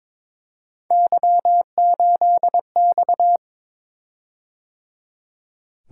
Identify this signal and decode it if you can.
Morse code, heard as a keyed tone, about 22 wpm, 700 Hz. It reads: Y8X
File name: practice / 22wpm / Three Letter-Number Combinations - 22wpm